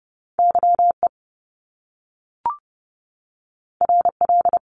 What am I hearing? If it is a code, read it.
Morse code, heard as a keyed tone, about 30 wpm, 700 Hz. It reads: YE RL
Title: Top Two Letter Combinations in Words 30wpm